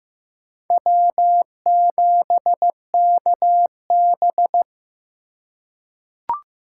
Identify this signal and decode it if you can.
Morse code, heard as a keyed tone, about 15 wpm, 700 Hz. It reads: W7KB
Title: USA Call Signs - 4 Characters - 3x Character Spacing - 15wpm